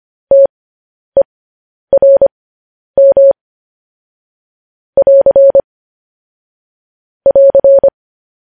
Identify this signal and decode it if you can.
Morse code, heard as a keyed tone, about 25 wpm, spaced as for 10 wpm, 550 Hz. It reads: TERM <AR> <AR>
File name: podcast / News headlines-1x-25x10